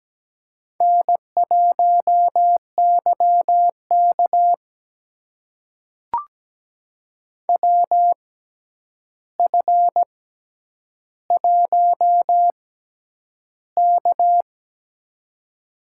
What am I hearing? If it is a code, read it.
Morse code, heard as a keyed tone, about 17 wpm, spaced as for 9 wpm, 700 Hz. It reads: N1YK WF1K
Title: USA Call Signs - 4 Characters - 6x Character Spacing - 17wpm